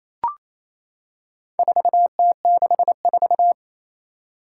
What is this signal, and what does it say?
Morse code, heard as a keyed tone, about 27 wpm, 700 Hz. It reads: 4T64
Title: Four Challenging Letter-Number Combinations 28wpm